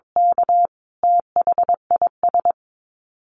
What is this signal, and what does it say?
Morse code, heard as a keyed tone, about 22 wpm, 700 Hz. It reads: K THIS
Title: Dit Challenge 22wpm